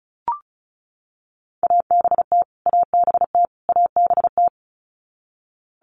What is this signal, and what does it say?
Morse code, heard as a keyed tone, about 34 wpm, 700 Hz. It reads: ABT ABT ABT
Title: Abbreviations Repeated 3x - 35wpm